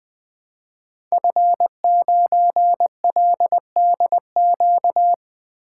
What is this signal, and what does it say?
Morse code, heard as a keyed tone, about 20 wpm, 700 Hz. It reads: F9LDQ
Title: Code Groups - 5 Letter:Numbers - 20wpm